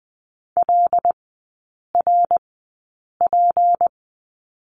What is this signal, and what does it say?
Morse code, heard as a keyed tone, about 20 wpm, spaced as for 8 wpm, 700 Hz. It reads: LRP
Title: 159 Lesson 26e - Letter g - Three Characters 20wpm